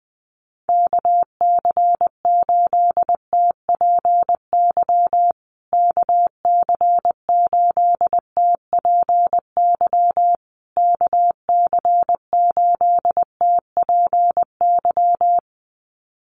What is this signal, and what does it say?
Morse code, heard as a keyed tone, about 20 wpm, 700 Hz. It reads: KC8TPY KC8TPY KC8TPY
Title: USA Call Signs Repeated 3x 20wpm